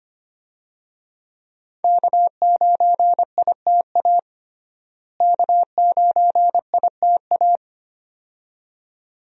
Morse code, heard as a keyed tone, about 25 wpm, 700 Hz. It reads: K9ITA K9ITA
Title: USA Call Signs - 5 Characters - Repeated 2x - 25wpm